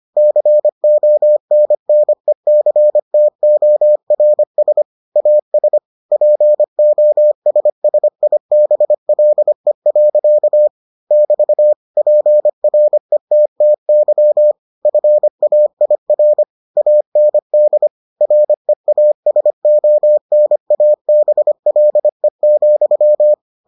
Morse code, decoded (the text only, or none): CONNECTORS AS POSSIBLE. <BT> PRETTY FAIR AND REASONABLE,